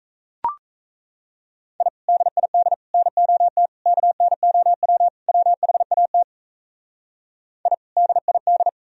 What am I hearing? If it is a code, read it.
Morse code, heard as a keyed tone, about 34 wpm, 700 Hz. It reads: I DID NOT KNOW WHAT I DID
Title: Speed-Racing - Sets of 5 Words 28wpm